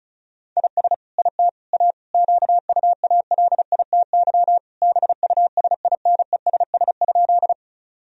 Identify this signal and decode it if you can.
Morse code, heard as a keyed tone, about 36 wpm, 700 Hz. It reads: IS IT A QUALITY BUSINESS?
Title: Sentences from Top 1000 Words - Review Part 4 - 35wpm